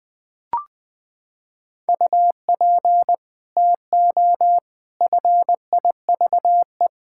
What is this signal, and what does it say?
Morse code, heard as a keyed tone, about 20 wpm, 700 Hz. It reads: UP TO FIVE
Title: Sets of 6 Words - 20wpm